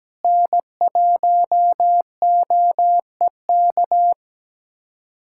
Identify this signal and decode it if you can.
Morse code, heard as a keyed tone, about 17 wpm, 700 Hz. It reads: N1OEK
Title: USA Callsign DE Callsign - 5 Characters - 17wpm